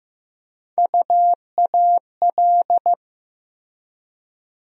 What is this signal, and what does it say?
Morse code, heard as a keyed tone, about 16 wpm, 700 Hz. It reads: UAL
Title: USA Call Sign Postfixes - 3 Letters 15wpm